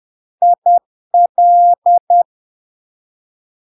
Morse code, heard as a keyed tone, about 10 wpm, 700 Hz. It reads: IL
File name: LARK-FD-2022-Log-in-CW-10wpm-700-Hz